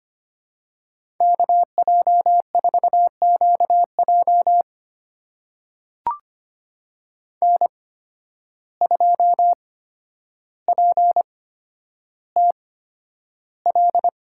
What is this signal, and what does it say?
Morse code, heard as a keyed tone, about 25 wpm, spaced as for 11 wpm, 700 Hz. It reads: KJ4QJ N2PTL
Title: USA Call Signs - 5 Characters - 8x Character Spacing - 25wpm